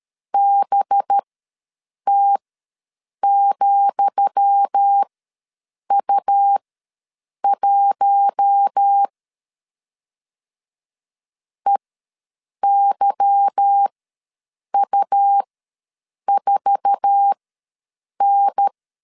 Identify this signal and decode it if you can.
Morse code, heard as a keyed tone, about 13 wpm, spaced as for 7 wpm, 800 Hz. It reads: BT,U1 EYU4N
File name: LEZIONI / 10 SIGNS / 10D COMA ws 12wpm 3618